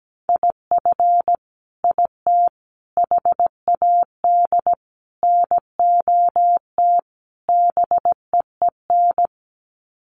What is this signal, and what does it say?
Morse code, heard as a keyed tone, about 17 wpm, 700 Hz. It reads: IF IT HAD NOT BEEN
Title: Sets of 5 Words Encore 17wpm